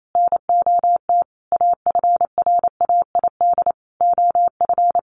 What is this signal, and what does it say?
Morse code, heard as a keyed tone, about 28 wpm, 700 Hz. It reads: NOT AFRAID OF
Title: Sentences from Top 1500 Words 28wpm